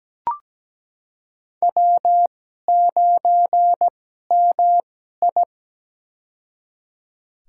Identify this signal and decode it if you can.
Morse code, heard as a keyed tone, about 17 wpm, spaced as for 12 wpm, 700 Hz. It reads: W9MI
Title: USA Call Signs - 2x Character Spacing - 17wpm